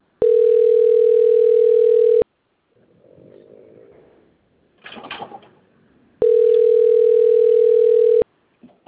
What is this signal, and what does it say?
Ringback tone (North America), 440 + 480 Hz.
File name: out- - - - - .23717